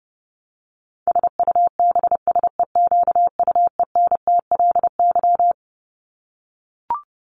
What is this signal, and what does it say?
Morse code, heard as a keyed tone, about 30 wpm, 700 Hz. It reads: SUBSEQUENTLY
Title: LY Suffix - Top 100 Words with Base Words 30wpm